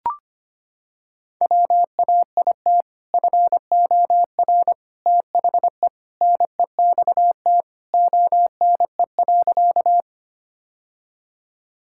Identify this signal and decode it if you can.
Morse code, heard as a keyed tone, about 25 wpm, 700 Hz. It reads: WAIT FOR THE NEXT ONE.
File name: practice / 25wpm / Sentences with 2-Word Phrases - 25wpm